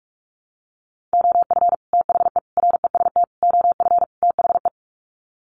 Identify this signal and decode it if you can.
Morse code, heard as a keyed tone, about 44 wpm, 700 Hz. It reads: OF THE REST OF THE
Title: Sets of 5 Words 45wpm